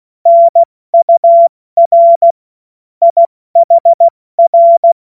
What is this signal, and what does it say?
Morse code, heard as a keyed tone, about 16 wpm, 680 Hz. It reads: NUR IHR
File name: Morsekurs / Witze / Witz 06. Dezember 2022 70 F80